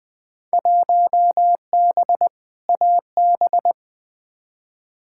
Morse code, heard as a keyed tone, about 20 wpm, 700 Hz. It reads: 1B AB
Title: Field Day Exchanges - 20wpm